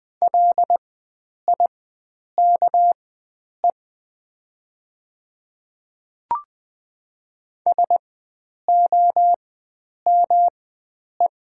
Morse code, heard as a keyed tone, about 20 wpm, spaced as for 9 wpm, 700 Hz. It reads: LIKE SOME
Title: 4x Character Spacing - Top 100 Words - 20wpm